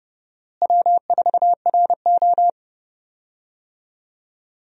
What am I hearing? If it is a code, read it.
Morse code, heard as a keyed tone, about 30 wpm, 700 Hz. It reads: W4RO
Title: USA Call Signs - 4 Characters - Repeated 2x - 30wpm